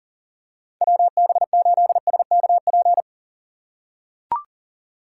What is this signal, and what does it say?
Morse code, heard as a keyed tone, about 40 wpm, 700 Hz. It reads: WB8SKP
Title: K1USN SST QSO Practice - 40wpm